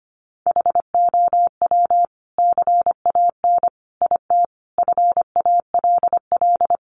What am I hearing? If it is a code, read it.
Morse code, heard as a keyed tone, about 25 wpm, 700 Hz. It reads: HOW CAN IT FALL
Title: Sentences from Top 1100 Words 25wpm